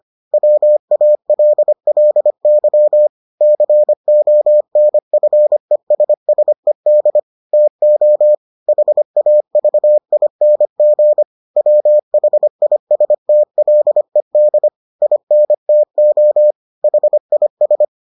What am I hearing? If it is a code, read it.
Morse code, heard as a keyed tone, about 25 wpm, 600 Hz. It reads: WALLY CONFESSED TO HAVING WHISTLED INTO HIS